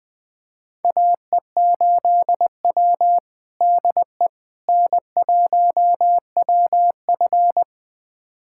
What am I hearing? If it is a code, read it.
Morse code, heard as a keyed tone, about 21 wpm, 700 Hz. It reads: AE8W DE N1WF
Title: USA Callsign DE Callsign - 4 Characters - 20wpm